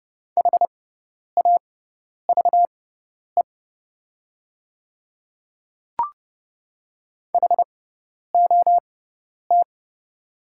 Morse code, heard as a keyed tone, about 30 wpm, spaced as for 10 wpm, 700 Hz. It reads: HAVE HOT
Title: 6x Character Spacing - Top 100 Words - 30wpm